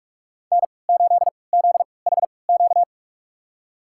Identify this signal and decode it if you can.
Morse code, heard as a keyed tone, about 46 wpm, spaced as for 26 wpm, 700 Hz. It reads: N8ZHQ
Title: USA Call Signs - 5 Characters - 3x Character Spacing - 45wpm